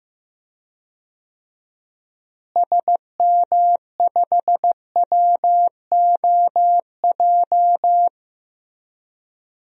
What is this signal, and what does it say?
Morse code, heard as a keyed tone, about 15 wpm, 700 Hz. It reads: SM5WOJ
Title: DXCC-100 Call Signs - 6 Characters - Repeated 3x - 15wpm